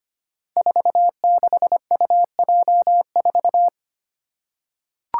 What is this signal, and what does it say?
Morse code, heard as a keyed tone, about 25 wpm, 700 Hz. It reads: 46UJ4
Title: Five Challenging Letter-Number Combinations 25wpm